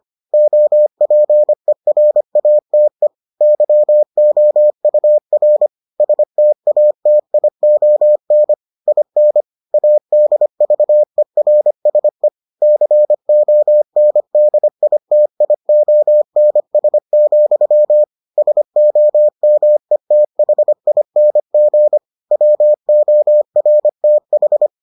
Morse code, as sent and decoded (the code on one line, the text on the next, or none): --- .--. .-. .- - . / -.-- --- ..- .-. / ... - .- - .. --- -. / .. -. / .- -.. ...- . .-. ... / -.-. --- -. -.. .. - .. --- -. ... --..-- / ... --- -- . - .... .. -. --. / .-- --- .-. - ....
OPERATE YOUR STATION IN ADVERSE CONDITIONS, SOMETHING WORTH